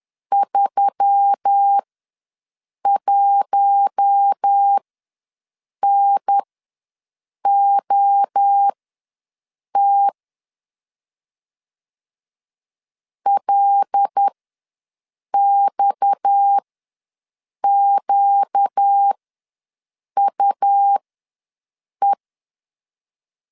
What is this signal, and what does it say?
Morse code, heard as a keyed tone, about 10 wpm, spaced as for 6 wpm, 800 Hz. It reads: 31NOT LXQUE
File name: LEZIONI / 6 PKX0 / 6 PKX ws 10wpm 3706